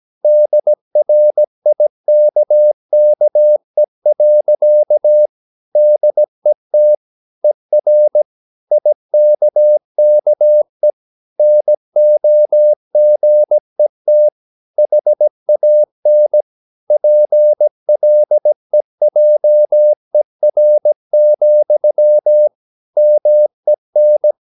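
Morse code, heard as a keyed tone, about 17 wpm, 600 Hz. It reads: DRIKKE. DET ER IKKE NOGET HAN PLEJER, MEN
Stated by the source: Slow (QRS) Morse practice at 17 wpm, from CW med Gnister.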